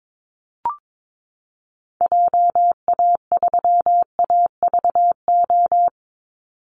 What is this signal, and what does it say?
Morse code, heard as a keyed tone, about 22 wpm, 700 Hz. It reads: JA3AVO
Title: Call Signs Repeated 3x - 22wpm